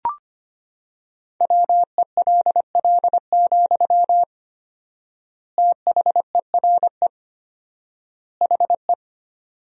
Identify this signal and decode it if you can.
Morse code, heard as a keyed tone, about 25 wpm, 700 Hz. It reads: WELL, THERE HE
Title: 4x Word-Spacing - Sentences from Top 500 Words - 25wpm